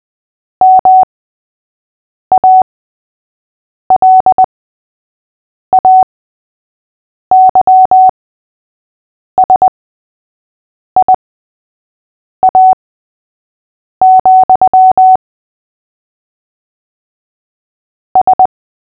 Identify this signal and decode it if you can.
Morse code, heard as a keyed tone, about 20 wpm, spaced as for 6 wpm, 750 Hz. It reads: MALAYSIA, S
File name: cw / fists / Short-Story-3-6wpm